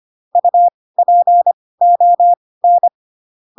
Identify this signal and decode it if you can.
Morse code, heard as a keyed tone, about 25 wpm, spaced as for 18 wpm, 700 Hz. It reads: UPON
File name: Pinocchio / Pinocchio ch5 12 25 ik1hge